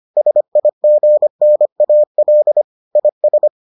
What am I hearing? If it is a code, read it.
Morse code, heard as a keyed tone, about 25 wpm, 600 Hz. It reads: SIGNAL IS